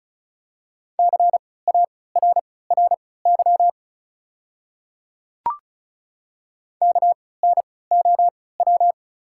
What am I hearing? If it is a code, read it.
Morse code, heard as a keyed tone, about 34 wpm, spaced as for 20 wpm, 690 Hz. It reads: CARRY KNOW
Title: 3x Character Spacing - Top 500 Words - 35wpm